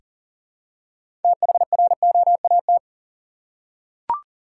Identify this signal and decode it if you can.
Morse code, heard as a keyed tone, about 40 wpm, 700 Hz. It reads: THROAT
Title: Six Letter Words - 40wpm